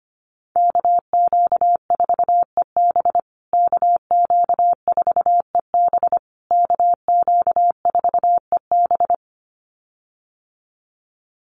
Morse code, heard as a keyed tone, about 25 wpm, 700 Hz. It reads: KQ4EB KQ4EB KQ4EB
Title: USA Call Signs Repeated 3x 25wpm